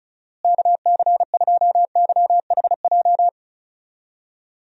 Morse code, heard as a keyed tone, about 34 wpm, 700 Hz. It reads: KC2YHJ
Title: USA Call Signs - 6 Characters - Repeated 3x - 35wpm